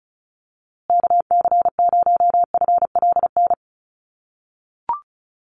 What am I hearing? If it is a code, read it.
Morse code, heard as a keyed tone, about 34 wpm, 700 Hz. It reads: KC0FLN
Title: USA Call Signs Repeated 3x 35wpm